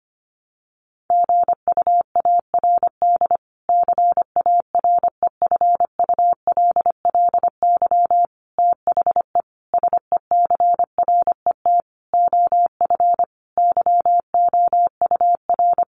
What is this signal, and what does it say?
Morse code, heard as a keyed tone, about 25 wpm, 700 Hz. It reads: GUARD CAREFULLY THE SECRET OF YOUR
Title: Sentences from Top 1300 Words 25wpm